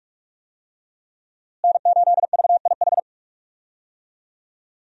Morse code, heard as a keyed tone, about 46 wpm, 700 Hz. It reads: N8VIH
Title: USA Call Signs - 5 Characters - Repeated 2x - 45wpm